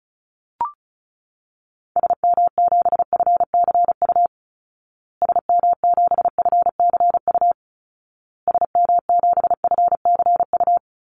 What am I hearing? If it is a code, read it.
Morse code, heard as a keyed tone, about 34 wpm, 700 Hz. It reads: SM7FCU SM7FCU SM7FCU
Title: Call Signs Repeated 2x - 35wpm